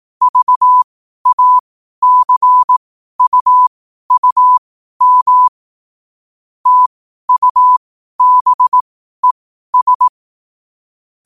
Morse code, heard as a keyed tone, about 18 wpm, spaced as for 12 wpm, 1000 Hz. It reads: VACUUM TUBES
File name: qotdfiles / 12wpm